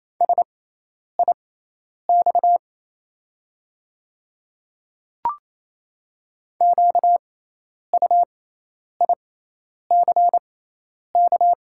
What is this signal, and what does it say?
Morse code, heard as a keyed tone, about 27 wpm, spaced as for 10 wpm, 700 Hz. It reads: SIX QUICK
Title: 6x Character Spacing - Top 500 Words - 28wpm